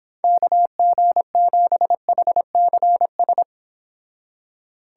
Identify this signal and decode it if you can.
Morse code, heard as a keyed tone, about 30 wpm, 700 Hz. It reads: KG7HCS